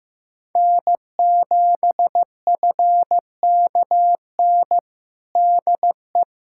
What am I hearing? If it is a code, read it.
Morse code, heard as a keyed tone, about 15 wpm, 700 Hz. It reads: N7FKN DE
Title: Practice QSO - Exchange 4 - Ending - 15wpm